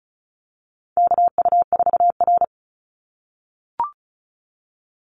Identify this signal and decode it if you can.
Morse code, heard as a keyed tone, about 34 wpm, 700 Hz. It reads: KU4R
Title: NA SOTA Chaser Callsigns 35wpm